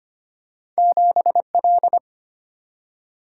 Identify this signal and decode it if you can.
Morse code, heard as a keyed tone, about 25 wpm, 700 Hz. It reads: 7L
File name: Two Letter-Number Combinations - 25wpm